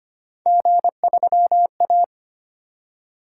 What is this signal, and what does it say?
Morse code, heard as a keyed tone, about 25 wpm, 700 Hz. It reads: G3A
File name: Three Letter-Number Combinations - 25wpm